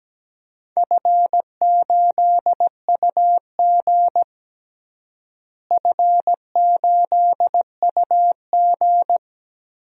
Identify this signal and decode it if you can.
Morse code, heard as a keyed tone, about 17 wpm, 700 Hz. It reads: F8UG F8UG
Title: DXCC-100 Call Signs - 4 Characters - Repeated 2x - 17wpm